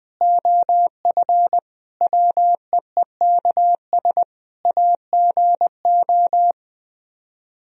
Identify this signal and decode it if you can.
Morse code, heard as a keyed tone, about 20 wpm, 700 Hz. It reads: OF WEEKS AGO
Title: Speed-Racing - Sets of 5 Words 20wpm